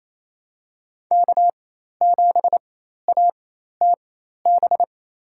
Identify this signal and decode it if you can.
Morse code, heard as a keyed tone, about 28 wpm, spaced as for 13 wpm, 700 Hz. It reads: K7ATB
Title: USA Call Signs - 4x Character Spacing - 28wpm